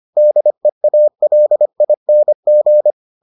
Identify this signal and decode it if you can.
Morse code, heard as a keyed tone, about 25 wpm, 600 Hz. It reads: DEALING